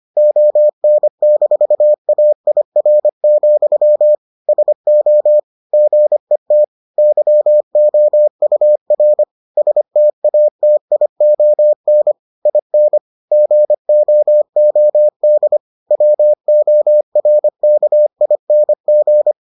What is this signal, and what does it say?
Morse code, heard as a keyed tone, about 25 wpm, 600 Hz. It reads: ON-AIR, SO GET YOUR STATION IN GOOD WORKING